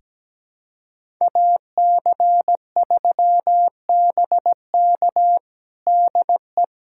Morse code, heard as a keyed tone, about 17 wpm, 700 Hz. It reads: AC3BK DE
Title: USA Callsign DE Callsign - 5 Characters - 17wpm